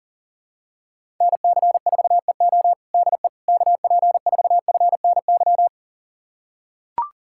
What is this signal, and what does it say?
Morse code, heard as a keyed tone, about 40 wpm, 700 Hz. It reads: NC4EO DE KP4FNY
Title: USA Callsign DE Callsign 40wpm